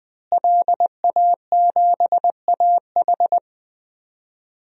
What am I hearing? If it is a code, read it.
Morse code, heard as a keyed tone, about 20 wpm, 700 Hz. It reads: LA7AH
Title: DXCC-100 Call Signs - 5 Characters - Repeated 2x - 20wpm